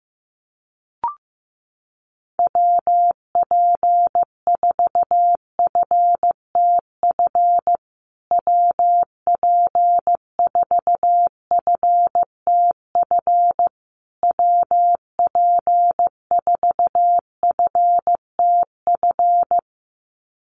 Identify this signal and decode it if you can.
Morse code, heard as a keyed tone, about 15 wpm, 700 Hz. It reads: WP4FTF WP4FTF WP4FTF
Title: USA Call Signs Repeated 3x 15wpm